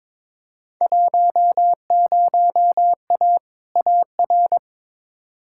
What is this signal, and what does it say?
Morse code, heard as a keyed tone, about 22 wpm, 700 Hz. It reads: 10A AR
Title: Field Day Exchanges Encore - 22wpm